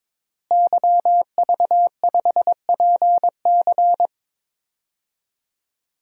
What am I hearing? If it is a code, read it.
Morse code, heard as a keyed tone, about 22 wpm, 700 Hz. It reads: YV5PC
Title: DXCC-100 Call Signs - 5 Characters - Repeated 3x - 22wpm